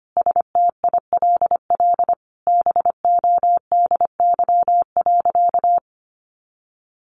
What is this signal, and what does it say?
Morse code, heard as a keyed tone, about 25 wpm, 700 Hz. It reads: STILL BODY.
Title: A Sound Of Thunder 25wpm